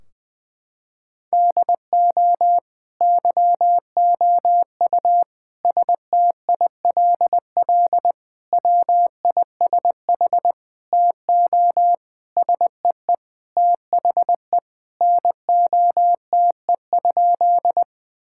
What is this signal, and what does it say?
Morse code, heard as a keyed tone, about 21 wpm, 700 Hz. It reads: DO YOU STILL WISH TO SEE THE NOTE?
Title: Sentences from Top 1600 Words 20wpm